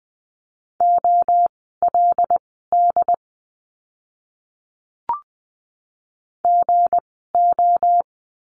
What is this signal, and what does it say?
Morse code, heard as a keyed tone, about 20 wpm, spaced as for 15 wpm, 700 Hz. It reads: OLD GO
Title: 237 Lesson 38f - Letter q - Words 20wpm